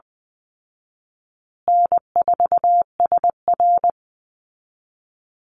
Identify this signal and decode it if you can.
Morse code, heard as a keyed tone, about 20 wpm, 700 Hz. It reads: N4SR
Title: Easy Call Signs Repeated 2x - 20wpm